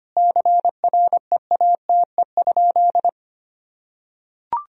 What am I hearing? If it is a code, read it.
Morse code, heard as a keyed tone, about 25 wpm, 700 Hz. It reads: CREATE?
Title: Sentences from Top 1000 Words - Review Part 3 - 25wpm